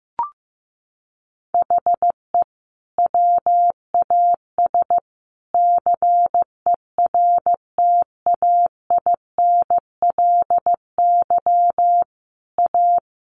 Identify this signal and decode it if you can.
Morse code, heard as a keyed tone, about 15 wpm, 700 Hz. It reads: HE WAS CERTAINLY A
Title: Sentences from Top 1200 Words 15wpm